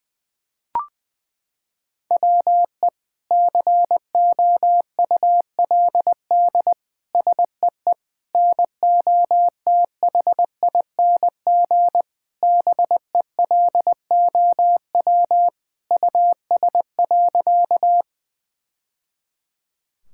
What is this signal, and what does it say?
Morse code, heard as a keyed tone, about 20 wpm, 700 Hz. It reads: WE COULD SEE NOTHING BELOW US.
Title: Sentences from Top 1000 Words - Review Part 5 - 20wpm